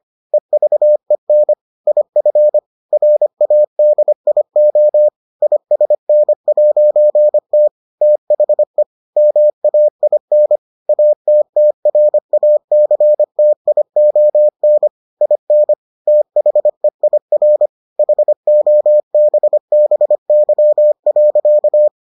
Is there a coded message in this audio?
Yes: Morse code from 0.3 to 21.9 s, heard as a keyed tone, about 25 wpm, 600 Hz. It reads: EVEN IF RADIO ISN'T THE MAIN ATTRACTION IN THEIR HOBBY.